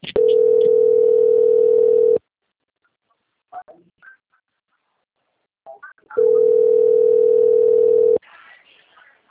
Ringback tone (North America), 440 + 480 Hz.